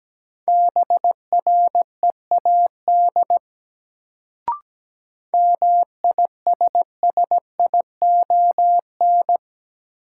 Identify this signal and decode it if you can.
Morse code, heard as a keyed tone, about 17 wpm, 700 Hz. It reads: BREAD MISSION
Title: Sentences from Top 1600 Words - New Word Drill - 17wpm